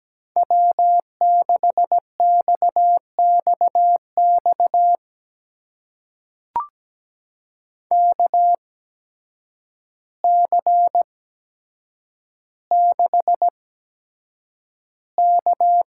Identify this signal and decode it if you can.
Morse code, heard as a keyed tone, about 17 wpm, spaced as for 8 wpm, 700 Hz. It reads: W6XXX KC6K
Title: USA Call Signs - 8x Character Spacing - 17wpm